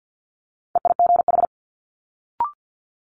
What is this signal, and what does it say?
Morse code, heard as a keyed tone, about 50 wpm, 700 Hz. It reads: EIGH
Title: Multi-letter Phonemes Quick Fire 50wpm